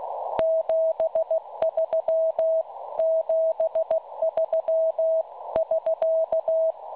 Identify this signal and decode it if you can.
Morse code, heard as a keyed tone, about 16 wpm, 650 Hz. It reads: 73 73 <SK>
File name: cwronde / websdr recording .6kHz